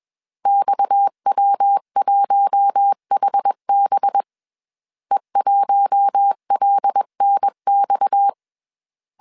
Morse code, heard as a keyed tone, about 21 wpm, 800 Hz. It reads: XW1HB E1LNX
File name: CW / DET / 1 6 1500C S C 20wpm 2116